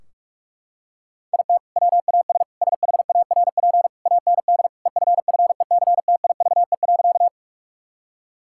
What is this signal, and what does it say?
Morse code, heard as a keyed tone, about 41 wpm, 700 Hz. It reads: IT WAS SHARP AND EFFECTIVE.
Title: Sentences from Top 1600 Words 45wpm